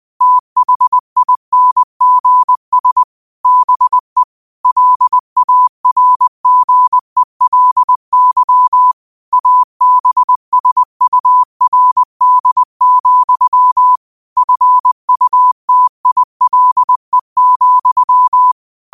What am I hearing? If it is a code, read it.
Morse code, heard as a keyed tone, about 20 wpm, 1000 Hz. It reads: THINGS BE LARGELY ABSURD, FUTILE,